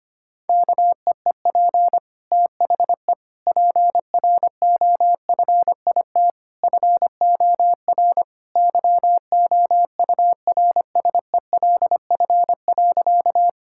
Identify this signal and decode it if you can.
Morse code, heard as a keyed tone, about 25 wpm, 700 Hz. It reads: KEEP THE PROFIT FOR YOURSELF.
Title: Sentences from Top 1000 Words - Review Part 5 - 25wpm